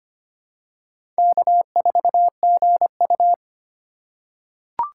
Morse code, heard as a keyed tone, about 25 wpm, 700 Hz. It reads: K4GU
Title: USA Call Signs - 8x Character Spacing - 25wpm